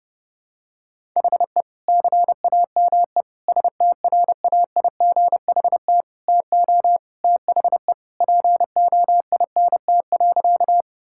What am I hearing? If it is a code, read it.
Morse code, heard as a keyed tone, about 30 wpm, 700 Hz. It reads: HE CAME STRAIGHT TO THE POINT.
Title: Sentences from Top 1000 Words - Review Part 6 - 30wpm